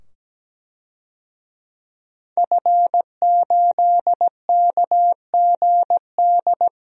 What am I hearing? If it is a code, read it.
Morse code, heard as a keyed tone, about 17 wpm, 700 Hz. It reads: F8KGD
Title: DXCC-100 Call Signs - 5 Characters - 17wpm